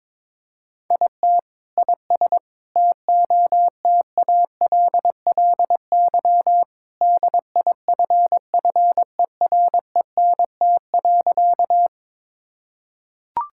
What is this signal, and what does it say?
Morse code, heard as a keyed tone, about 22 wpm, 700 Hz. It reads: IT IS TOTALLY DIFFERENT.
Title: Sentences from Top 1600 Words 22wpm